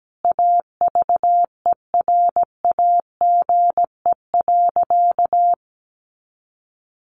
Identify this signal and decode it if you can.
Morse code, heard as a keyed tone, about 17 wpm, 700 Hz. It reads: AVERAGE.
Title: Sentences from Top 1400 Words 17wpm